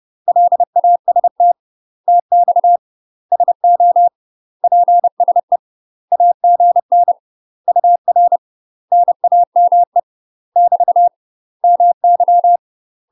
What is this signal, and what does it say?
Morse code, heard as a keyed tone, about 30 wpm, 700 Hz. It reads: LAST TX SO PSE AGN UR NAME <BT> MY
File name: UK MASTER QSOs TRAINING / 30WPM / UK MASTER QSOs 0013